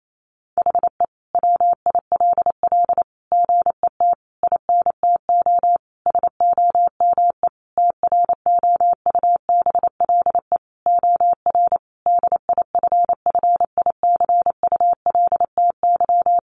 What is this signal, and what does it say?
Morse code, heard as a keyed tone, about 28 wpm, 700 Hz. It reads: HE WILL GET INTO SOME TROUBLE OR DIFFICULTY
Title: Sentences from Top 1500 Words 28wpm